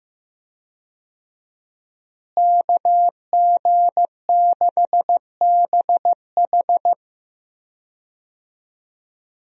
Morse code, heard as a keyed tone, about 15 wpm, 700 Hz. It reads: KG6BH
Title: USA Call Signs - 5 Characters - Repeated 2x - 15wpm